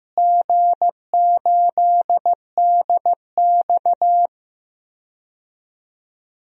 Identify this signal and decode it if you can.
Morse code, heard as a keyed tone, about 15 wpm, 700 Hz. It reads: G8DX
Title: DXCC-100 Call Signs - 4 Characters - Repeated 2x - 15wpm